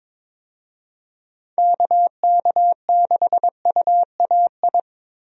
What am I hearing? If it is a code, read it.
Morse code, heard as a keyed tone, about 22 wpm, 700 Hz. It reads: KK6UAI
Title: USA Call Signs - 6 Characters - Repeated 3x - 22wpm